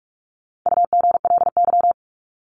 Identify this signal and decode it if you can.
Morse code, heard as a keyed tone, about 44 wpm, 700 Hz. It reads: UGLY
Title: Top 1751-2000 Words 45wpm